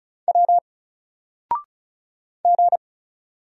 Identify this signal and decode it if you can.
Morse code, heard as a keyed tone, about 36 wpm, 700 Hz. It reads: W G
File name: Single-Letter-Number-35wpm